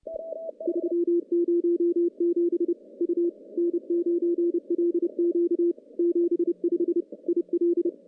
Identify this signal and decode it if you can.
Morse code, heard as a keyed tone, about 30 wpm, 340 Hz. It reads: 307 U N9LQ 75 IL
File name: CW-SS-k5tr